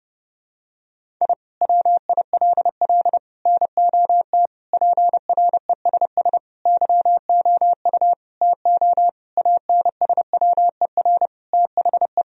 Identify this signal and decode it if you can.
Morse code, heard as a keyed tone, about 30 wpm, 700 Hz. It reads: I WILL NOT PRESS YOU TO ANSWER THE
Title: Sentences from Top 1000 Words - Review Part 1 - 30wpm